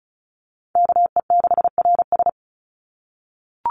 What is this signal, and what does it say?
Morse code, heard as a keyed tone, about 34 wpm, 700 Hz. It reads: KE6RS
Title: NA SOTA Chaser Callsigns 35wpm